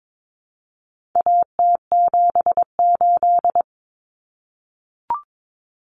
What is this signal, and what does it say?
Morse code, heard as a keyed tone, about 22 wpm, 700 Hz. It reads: AT78
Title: Common Cut Numbers - 4-Digit 22wpm